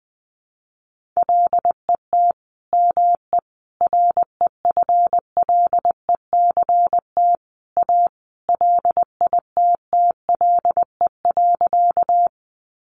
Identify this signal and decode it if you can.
Morse code, heard as a keyed tone, about 20 wpm, 700 Hz. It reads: LET ME REFLECT A LITTLE.
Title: Sentences from Top 1500 Words 20wpm